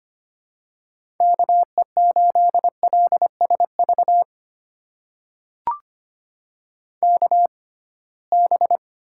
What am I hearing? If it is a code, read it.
Morse code, heard as a keyed tone, about 25 wpm, spaced as for 19 wpm, 700 Hz. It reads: KE8LSV KB
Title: USA Call Signs - 6 Characters - 6x Character Spacing - 25wpm